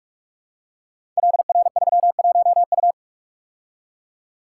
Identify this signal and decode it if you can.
Morse code, heard as a keyed tone, about 44 wpm, 700 Hz. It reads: LR31U
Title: Code Groups - 5 Letter:Numbers - 45wpm